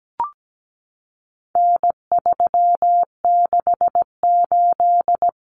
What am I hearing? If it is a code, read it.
Morse code, heard as a keyed tone, about 17 wpm, 700 Hz. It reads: N368
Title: Common Cut Numbers - 4-Digit 17wpm